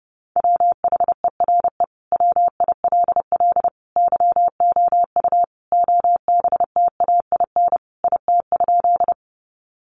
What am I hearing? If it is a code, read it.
Morse code, heard as a keyed tone, about 30 wpm, 700 Hz. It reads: WHERE WILL YOU OBTAIN IT?
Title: Sentences from Top 1300 Words 30wpm